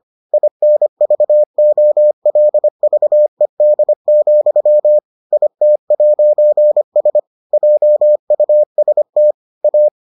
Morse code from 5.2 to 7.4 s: IT'S

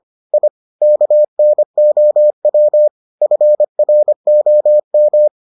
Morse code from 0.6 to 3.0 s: KNOW

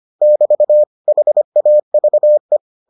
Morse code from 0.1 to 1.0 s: <BT>

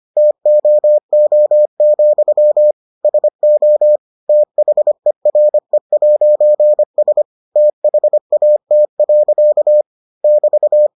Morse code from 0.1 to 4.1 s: TOO, SO

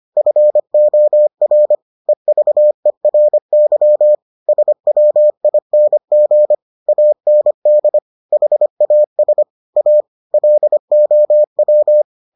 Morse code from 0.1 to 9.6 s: FOR EVERY SWING AND HAS

